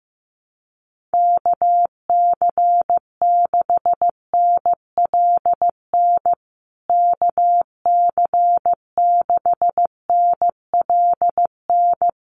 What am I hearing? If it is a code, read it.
Morse code, heard as a keyed tone, about 14 wpm, 700 Hz. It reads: KC6NLN KC6NLN
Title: USA Call Signs Repeated 3x 15wpm